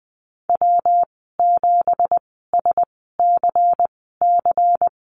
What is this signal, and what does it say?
Morse code, heard as a keyed tone, about 20 wpm, spaced as for 14 wpm, 700 Hz. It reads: W7SCC
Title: 213 Lesson 34g - Number 8 - Call Signs 20wpm